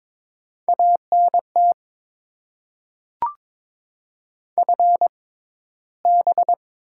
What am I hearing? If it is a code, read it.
Morse code, heard as a keyed tone, about 22 wpm, spaced as for 13 wpm, 700 Hz. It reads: ANT FB
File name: Top 20 QSO Elements - 6x Character Spacing - 22wpm